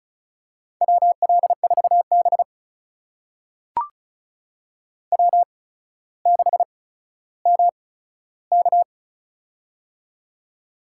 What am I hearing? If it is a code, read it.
Morse code, heard as a keyed tone, about 36 wpm, spaced as for 15 wpm, 700 Hz. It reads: WL4B W6MK
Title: USA Call Signs - 4 Characters - 8x Character Spacing - 35wpm